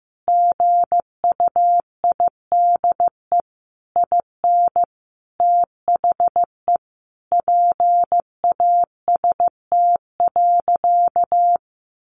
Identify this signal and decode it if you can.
Morse code, heard as a keyed tone, about 15 wpm, 700 Hz. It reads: GUIDE IN THE PAST.
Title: A Sound Of Thunder 15wpm